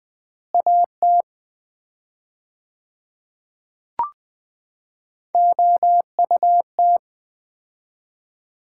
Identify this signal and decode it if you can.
Morse code, heard as a keyed tone, about 20 wpm, 700 Hz. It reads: AT OUT
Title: 4x Word-Spacing - Sets of 3 Words - 20wpm